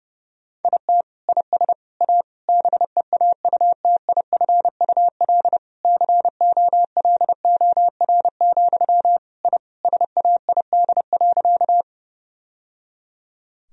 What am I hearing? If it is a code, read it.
Morse code, heard as a keyed tone, about 30 wpm, 700 Hz. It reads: IT IS A BEAUTIFUL COLOR, I SAID.
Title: Sentences from Top 1000 Words - Review Part 2 - 30wpm